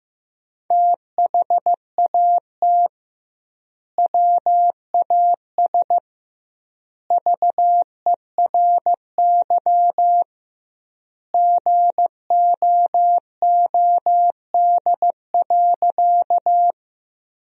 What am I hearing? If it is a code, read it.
Morse code, heard as a keyed tone, about 15 wpm, 700 Hz. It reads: THAT WAS VERY GOOD.
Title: 2x Word-Spacing - Sentences from Top 200 Words - 15wpm